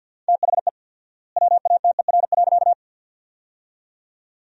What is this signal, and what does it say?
Morse code, heard as a keyed tone, about 48 wpm, 700 Hz. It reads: THE WATER.
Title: 4x Word-Spacing - Sentences from Top 300 Words - 50wpm